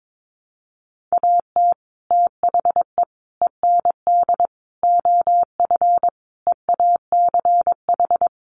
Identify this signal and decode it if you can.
Morse code, heard as a keyed tone, about 22 wpm, 700 Hz. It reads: AT THE END OF EACH
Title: Sets of 5 Words Encore 22wpm